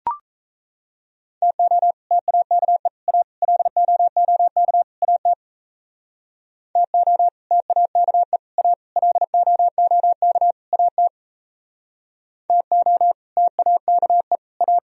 Morse code, heard as a keyed tone, about 33 wpm, 700 Hz. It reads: TO TAKE A LOOK AT TO TAKE A LOOK AT TO TAKE A
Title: Speed-Racing - Sets of 5 Words 28wpm